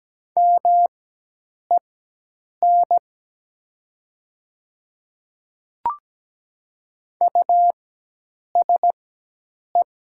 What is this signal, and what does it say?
Morse code, heard as a keyed tone, about 17 wpm, spaced as for 8 wpm, 700 Hz. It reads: MEN USE
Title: 4x Character Spacing - Top 200 Words - 17wpm